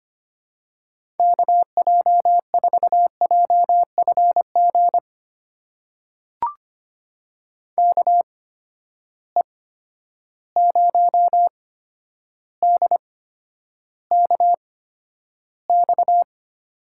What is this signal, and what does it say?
Morse code, heard as a keyed tone, about 25 wpm, spaced as for 11 wpm, 700 Hz. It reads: KJ4JFG KE0DKX
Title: USA Call Signs - 6 Characters - 8x Character Spacing - 25wpm